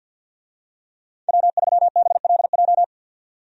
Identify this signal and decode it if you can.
Morse code, heard as a keyed tone, about 49 wpm, 700 Hz. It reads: W3BLJ
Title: USA Call Signs - 5 Characters - 50wpm